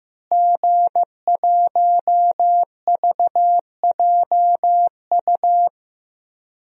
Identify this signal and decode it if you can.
Morse code, heard as a keyed tone, about 15 wpm, 700 Hz. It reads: G1VJU
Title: DXCC-100 Call Signs - 5 Characters - Repeated 3x - 15wpm